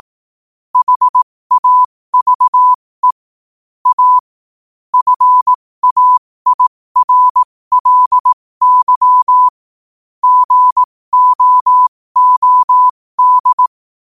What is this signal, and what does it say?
Morse code, heard as a keyed tone, about 18 wpm, spaced as for 15 wpm, 1000 Hz. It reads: HAVE A FAIRLY GOOD
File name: qotdfiles / 15wpm